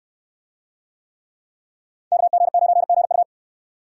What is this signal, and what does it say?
Morse code, heard as a keyed tone, about 67 wpm, 700 Hz. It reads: KG9GU